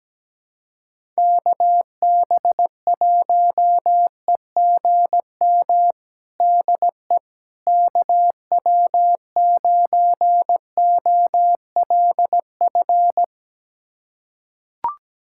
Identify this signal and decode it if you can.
Morse code, heard as a keyed tone, about 17 wpm, 700 Hz. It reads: KB1EGM DE KW9OLF
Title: USA Callsign DE Callsign 17wpm